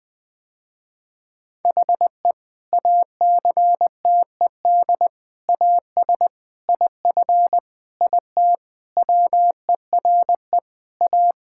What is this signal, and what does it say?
Morse code, heard as a keyed tone, about 20 wpm, 700 Hz. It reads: HE ACTED AS IF IT WERE A
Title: Sentences with 5-Word Phrases - Spoken Beginning - 20wpm